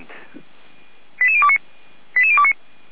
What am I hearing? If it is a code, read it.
Five-tone selective call (ZVEI), heard as quick runs of five tones, tones of about 70 ms. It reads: 88029 88029